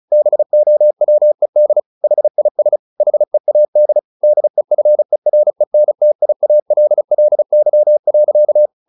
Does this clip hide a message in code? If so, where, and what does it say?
Yes: Morse code from 0.1 to 8.7 s, heard as a keyed tone, about 34 wpm, 600 Hz. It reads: BOWED HIS HEAD DEFERENTIALLY.